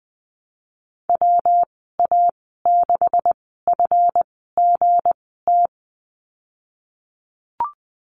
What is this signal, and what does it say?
Morse code, heard as a keyed tone, about 19 wpm, spaced as for 14 wpm, 700 Hz. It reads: WA6FGT
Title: 179 Lesson 29g - Slash - Call Signs 20wpm